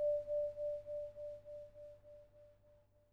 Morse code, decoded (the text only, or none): none